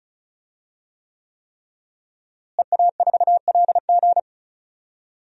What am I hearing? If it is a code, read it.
Morse code, heard as a keyed tone, about 34 wpm, 690 Hz. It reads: EA4LG
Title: DXCC-100 Call Signs - 5 Characters - Repeated 3x - 35wpm